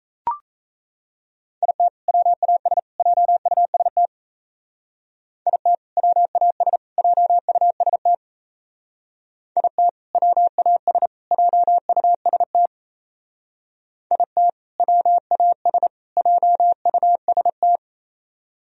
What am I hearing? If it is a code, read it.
Morse code, heard as a keyed tone, about 34 wpm, 700 Hz. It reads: IT WAS JUST IT WAS JUST IT WAS JUST IT WAS JUST
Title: Speed-Racing - Sets of 3 Words 28wpm